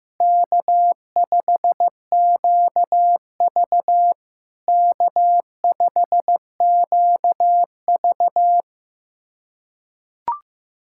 Morse code, heard as a keyed tone, about 15 wpm, 700 Hz. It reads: K5QV K5QV
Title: SOTA QSO Practice 15wpm